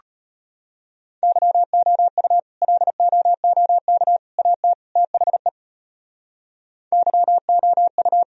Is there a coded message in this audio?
Yes: Morse code from 1.2 to 8.2 s, heard as a keyed tone, about 31 wpm, 700 Hz. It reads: YOU LOOK AT THE YOU